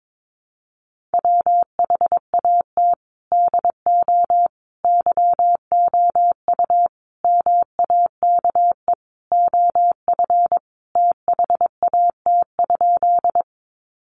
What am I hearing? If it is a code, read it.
Morse code, heard as a keyed tone, about 22 wpm, 700 Hz. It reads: WHAT DO YOU MAKE OF THAT?
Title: Sentences from Top 100 Words 22wpm